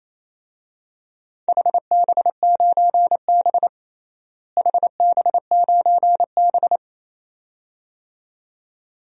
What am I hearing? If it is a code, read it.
Morse code, heard as a keyed tone, about 28 wpm, 700 Hz. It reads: HB9B HB9B
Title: DXCC-100 Call Signs - 4 Characters - Repeated 2x - 28wpm